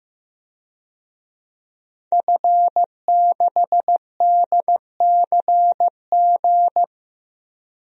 Morse code, heard as a keyed tone, about 15 wpm, 700 Hz. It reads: F6DCG
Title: DXCC-100 Call Signs - 5 Characters - Repeated 2x - 15wpm